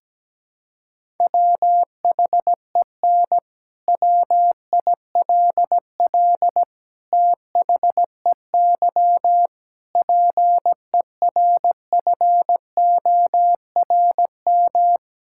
Morse code, heard as a keyed tone, about 17 wpm, 700 Hz. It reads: WHEN WILL THEY PERFORM
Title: Sentences from Top 1000 Words - Review Part 5 - 17wpm